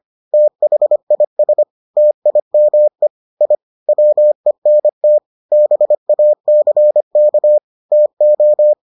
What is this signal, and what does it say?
Morse code, heard as a keyed tone, about 25 wpm, 600 Hz. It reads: THIS TIME I WENT BACK TO